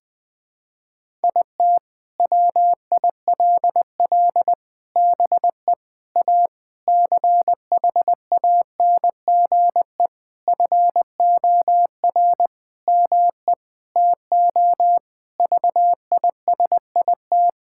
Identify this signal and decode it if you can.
Morse code, heard as a keyed tone, about 21 wpm, 700 Hz. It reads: IT WILL BE A CHANGE FOR ME TO VISIT
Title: Sentences from Top 1000 Words - Review Part 6 - 20wpm